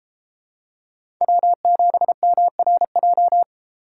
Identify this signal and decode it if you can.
Morse code, heard as a keyed tone, about 32 wpm, 700 Hz. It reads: W7MRJ